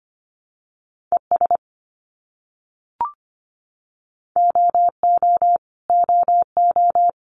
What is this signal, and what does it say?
Morse code, heard as a keyed tone, about 25 wpm, 700 Hz. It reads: ES OO OO
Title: Abbreviations Repeated 3x - 25wpm